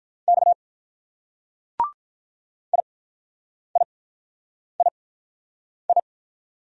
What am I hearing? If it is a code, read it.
Morse code, heard as a keyed tone, about 44 wpm, 700 Hz. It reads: X I I I I